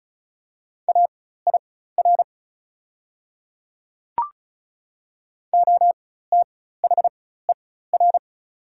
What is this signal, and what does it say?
Morse code, heard as a keyed tone, about 34 wpm, spaced as for 16 wpm, 700 Hz. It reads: AIR OTHER
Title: 4x Character Spacing - Top 300 Words - 35wpm